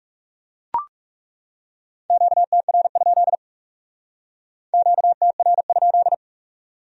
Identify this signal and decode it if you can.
Morse code, heard as a keyed tone, about 37 wpm, 700 Hz. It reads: QTR? QTR?
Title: Speed-Racing Q-Codes - 30wpm